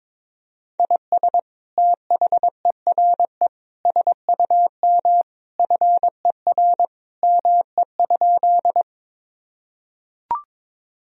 Morse code, heard as a keyed tone, about 22 wpm, 700 Hz. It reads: IS THERE SUM FER ME?
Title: Sentences from Top 200 Words and Abbreviations 22wpm